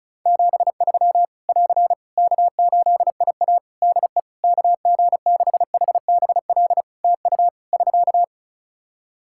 Morse code, heard as a keyed tone, about 34 wpm, 700 Hz. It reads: 73 <AR> K8IA DE KG6HBL TU <SK>
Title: Practice QSO - Exchange 4 - Ending - 35wpm